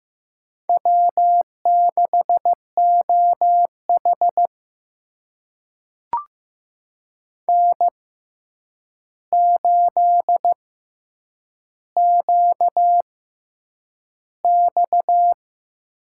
Morse code, heard as a keyed tone, about 15 wpm, spaced as for 8 wpm, 700 Hz. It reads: W6OH N8QX
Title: USA Call Signs - 4 Characters - 6x Character Spacing - 15wpm